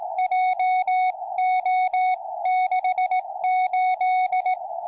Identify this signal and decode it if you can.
Morse code, heard as a keyed tone, about 17 wpm, 740 Hz. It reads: JO68